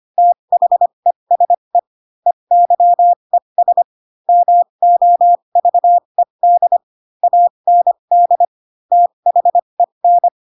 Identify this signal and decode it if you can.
Morse code, heard as a keyed tone, about 25 wpm, spaced as for 22 wpm, 700 Hz. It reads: THESE EYES MOVED AND THEN
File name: Pinocchio / Pinocchio ch3 18 25 ik1hge